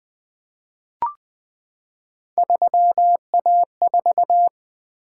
Morse code, heard as a keyed tone, about 20 wpm, 700 Hz. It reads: 3A4
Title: Three Letter-Number Combinations - 20wpm